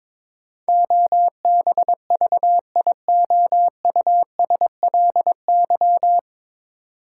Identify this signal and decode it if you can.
Morse code, heard as a keyed tone, about 21 wpm, 700 Hz. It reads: OBVIOUSLY
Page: Morse Code Ninja practice 22wpm